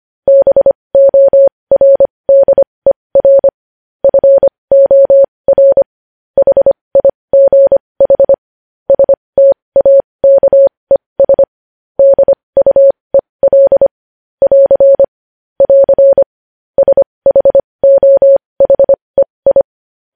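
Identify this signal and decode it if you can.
Morse code, heard as a keyed tone, about 25 wpm, spaced as for 20 wpm, 550 Hz. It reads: BORDER FOR HIGH STAKES DUEL <AR> <AR> SHOHEI
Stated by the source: News headlines in Morse at 20 wpm.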